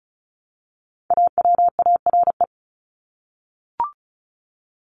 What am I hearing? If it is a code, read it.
Morse code, heard as a keyed tone, about 34 wpm, 700 Hz. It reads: AWARE
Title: Top 1001-1250 Words 35wpm